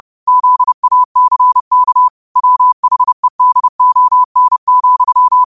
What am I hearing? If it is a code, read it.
Morse code, heard as a keyed tone, about 30 wpm, 1000 Hz. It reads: ZACK WHEDON,
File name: qotdfiles / 30wpm